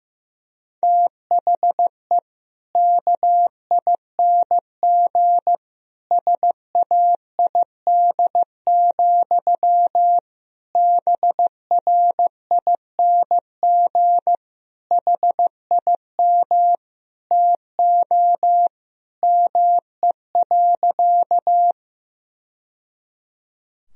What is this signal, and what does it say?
Morse code, heard as a keyed tone, about 16 wpm, 700 Hz. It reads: THE KING SAID, BRING HIM TO ME.
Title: Sentences from Top 1000 Words - Review Part 3 - 15wpm